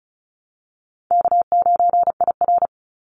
Morse code, heard as a keyed tone, about 34 wpm, 700 Hz. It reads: K9IR
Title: Easy Call Signs Repeated 2x - 35wpm